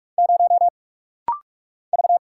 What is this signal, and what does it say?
Morse code, heard as a keyed tone, about 44 wpm, 700 Hz. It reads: 0 V
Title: Single Letter-Number Alt - Rapid-Fire 45wpm